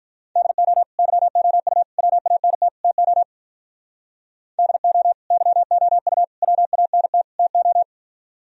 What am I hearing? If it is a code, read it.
Morse code, heard as a keyed tone, about 44 wpm, 700 Hz. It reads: DO YOU WANT TO DO YOU WANT TO
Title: Speed-Racing - Sets of 4 Words 35wpm